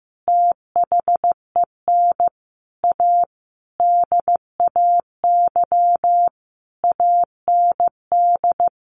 Morse code, heard as a keyed tone, about 15 wpm, 700 Hz. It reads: THEN A DAY AND
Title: A Sound Of Thunder 15wpm